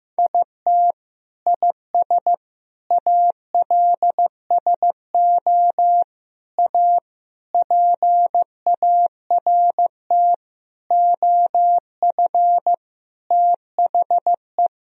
Morse code, heard as a keyed tone, about 16 wpm, 700 Hz. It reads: IT IS ALSO A PART OF THE
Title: Sentences from Top 1000 Words - Review Part 1 - 15wpm